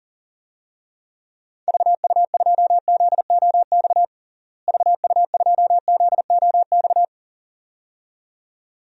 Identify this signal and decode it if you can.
Morse code, heard as a keyed tone, about 40 wpm, 700 Hz. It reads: VU2ZOX VU2ZOX
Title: DXCC-100 Call Signs - 6 Characters - Repeated 2x - 40wpm